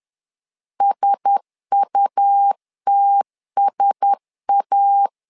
Morse code, heard as a keyed tone, about 10 wpm, 800 Hz. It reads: SUTSA
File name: LEZIONI / 4 Q1S 10wpm 2133